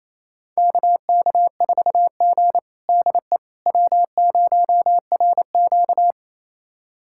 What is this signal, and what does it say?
Morse code, heard as a keyed tone, about 28 wpm, 700 Hz. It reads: KK4G DE W0RQ
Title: USA Callsign DE Callsign - 4 Characters - 28wpm